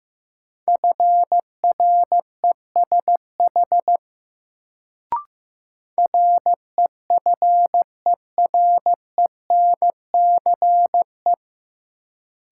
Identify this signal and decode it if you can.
Morse code, heard as a keyed tone, about 15 wpm, 700 Hz. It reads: FRESH REFERENCE
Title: Sentences from Top 1200 Words - New Word Drill - 15wpm